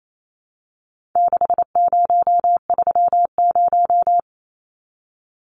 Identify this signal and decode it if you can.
Morse code, heard as a keyed tone, about 28 wpm, 700 Hz. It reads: 6030
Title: Numbers - 4-Digit 28wpm